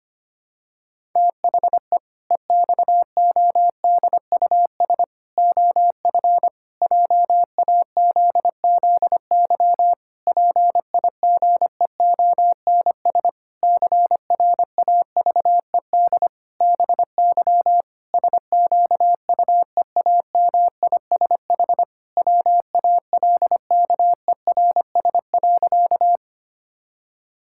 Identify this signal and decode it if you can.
Morse code, heard as a keyed tone, about 25 wpm, 700 Hz. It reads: THE EXODUS OF JAZZY PIGEONS CRAVED BY SQUEAMISH WALKERS.
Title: Pangram Challenge v2 - 25wpm